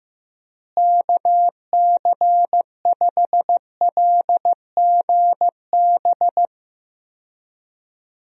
Morse code, heard as a keyed tone, about 15 wpm, 700 Hz. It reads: KC5LGB
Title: USA Call Signs - 6 Characters - Repeated 3x - 15wpm